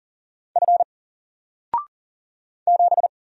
Morse code, heard as a keyed tone, about 40 wpm, 700 Hz. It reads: F 7
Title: Single Character (Letter:Number:Punctuation:Prosign) 40wpm